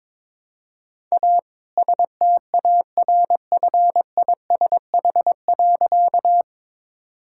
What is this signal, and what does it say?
Morse code, heard as a keyed tone, about 22 wpm, 690 Hz. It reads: A STARFISH.
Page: morse-code-ninja-practice-22wpm